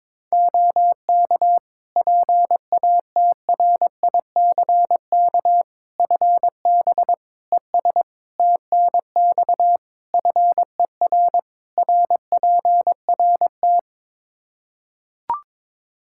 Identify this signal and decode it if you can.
Morse code, heard as a keyed tone, about 22 wpm, 700 Hz. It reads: OK PATRICK FB ES TNX FER RPRT
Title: Practice QSO - Exchange 2 - Info - 22wpm